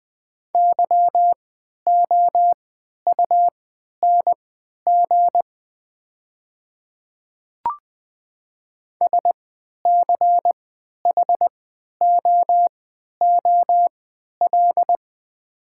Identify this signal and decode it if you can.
Morse code, heard as a keyed tone, about 20 wpm, spaced as for 11 wpm, 700 Hz. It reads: YOUNG SCHOOL
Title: 3x Character Spacing - Top 500 Words - 20wpm